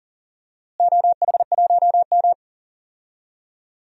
Morse code, heard as a keyed tone, about 37 wpm, 690 Hz. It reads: OH1M
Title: DXCC-100 Call Signs - 4 Characters - Repeated 2x - 40wpm